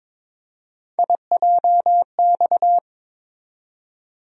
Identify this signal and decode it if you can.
Morse code, heard as a keyed tone, about 22 wpm, 700 Hz. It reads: IJX
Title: USA Call Sign Postfixes - 3 Letters 22wpm